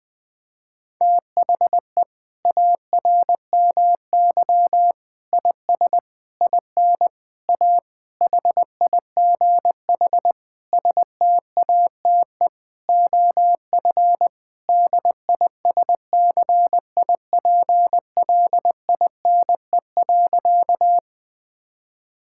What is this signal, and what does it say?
Morse code, heard as a keyed tone, about 19 wpm, 700 Hz. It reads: THE ARMY IS IN A HIGH STATE OF DISCIPLINE.
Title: Sentences from Top 1000 Words - Review Part 6 - 20wpm